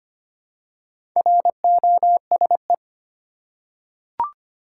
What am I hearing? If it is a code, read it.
Morse code, heard as a keyed tone, about 25 wpm, 700 Hz. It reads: ROSE
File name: practice / 25wpm / Easy Words with Hints - 25wpm